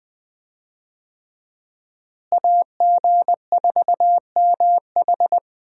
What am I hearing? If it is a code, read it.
Morse code, heard as a keyed tone, about 20 wpm, 700 Hz. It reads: AG4MH
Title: USA Call Signs - 5 Characters - Repeated 3x - 20wpm